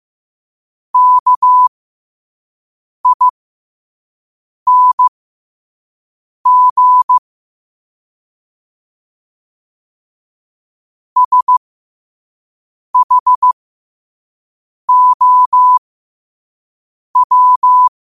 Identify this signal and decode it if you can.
Morse code, heard as a keyed tone, about 15 wpm, spaced as for 5 wpm, 1000 Hz. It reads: KING SHOW
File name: qotdfiles / 5wpm